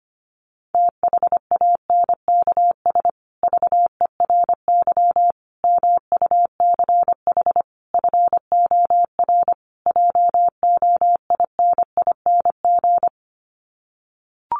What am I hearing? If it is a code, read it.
Morse code, heard as a keyed tone, about 25 wpm, 700 Hz. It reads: THANKS VERY MUCH FOR JOINING
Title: Sets of 5 Words Encore2 25wpm